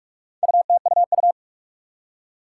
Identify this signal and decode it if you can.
Morse code, heard as a keyed tone, about 46 wpm, 700 Hz. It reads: UTUU
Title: Four Challenging Letter-Number Combinations 45wpm